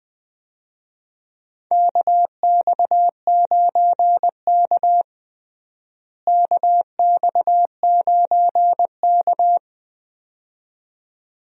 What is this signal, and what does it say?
Morse code, heard as a keyed tone, about 20 wpm, 700 Hz. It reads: KX9K KX9K
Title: USA Call Signs - 4 Characters - Repeated 2x - 20wpm